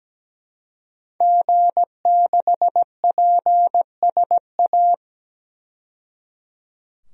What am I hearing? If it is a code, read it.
Morse code, heard as a keyed tone, about 17 wpm, 700 Hz. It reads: G6PSA
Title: DXCC-100 Call Signs - 5 Characters - 17wpm